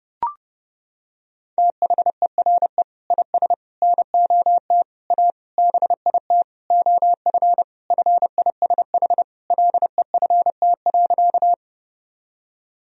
Morse code, heard as a keyed tone, about 30 wpm, 700 Hz. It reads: THERE IS NOT A BIT OF FISH LEFT.
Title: Sentences from Top 1000 Words - Review Part 1 - 30wpm